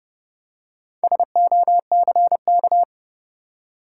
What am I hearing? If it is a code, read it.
Morse code, heard as a keyed tone, about 29 wpm, 700 Hz. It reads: SOCK
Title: Four Letter Words - 30wpm